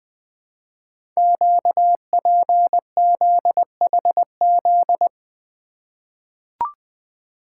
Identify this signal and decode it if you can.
Morse code, heard as a keyed tone, about 20 wpm, 700 Hz. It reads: QPZHZ
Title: Code Groups - 5 Letters - 20wpm